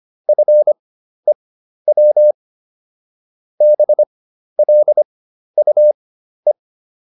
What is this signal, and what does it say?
Morse code, heard as a keyed tone, about 25 wpm, spaced as for 12 wpm, 600 Hz. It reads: FEW BLUE